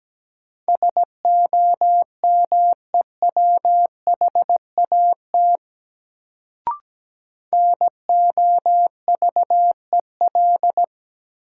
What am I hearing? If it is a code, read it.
Morse code, heard as a keyed tone, about 17 wpm, 700 Hz. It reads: SOMEWHAT NOVEL
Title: Sentences from Top 1300 Words - New Word Drill Alt - 17wpm